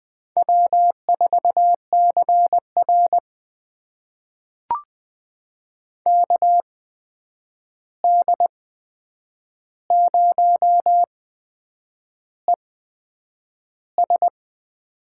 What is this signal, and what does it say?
Morse code, heard as a keyed tone, about 20 wpm, spaced as for 8 wpm, 700 Hz. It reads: W4CR KD0ES
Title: USA Call Signs - 8x Character Spacing - 20wpm